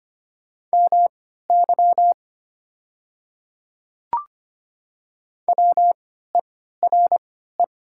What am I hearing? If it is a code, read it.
Morse code, heard as a keyed tone, about 25 wpm, spaced as for 14 wpm, 700 Hz. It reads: MY WERE
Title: 3x Character Spacing - Top 100 Words - 25wpm